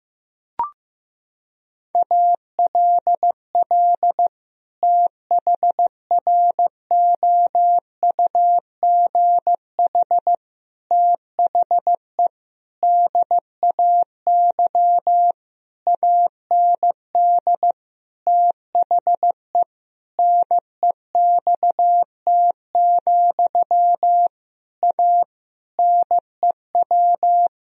Morse code, heard as a keyed tone, about 15 wpm, 700 Hz. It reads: ALL THROUGH THE DAY AND THE NEXT, A NEW
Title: Sentences from Top 1000 Words - Review Part 1 - 15wpm